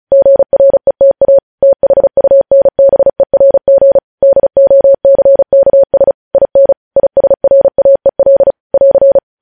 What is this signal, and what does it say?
Morse code, heard as a keyed tone, about 36 wpm, 550 Hz. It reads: GRETA THUNBERG DOCKS IN ISRAEL <AR>
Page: news headlines in morse code at 35 wpm